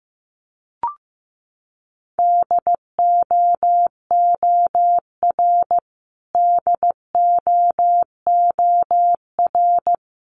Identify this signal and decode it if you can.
Morse code, heard as a keyed tone, about 15 wpm, 700 Hz. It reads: DOOR DOOR
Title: Top 500 Words Repeated 15wpm